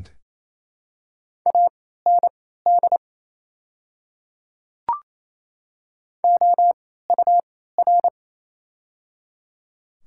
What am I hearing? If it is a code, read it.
Morse code, heard as a keyed tone, about 28 wpm, spaced as for 16 wpm, 700 Hz. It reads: AND OUR
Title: 3x Character Spacing - Top 200 Words - 28wpm